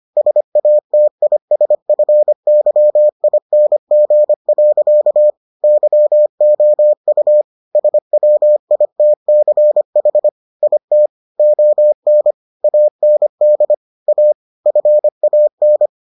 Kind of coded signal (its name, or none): Morse code